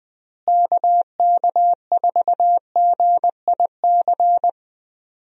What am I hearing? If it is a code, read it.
Morse code, heard as a keyed tone, about 20 wpm, 700 Hz. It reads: KK4GIC
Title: USA Call Signs - 6 Characters - Repeated 2x - 20wpm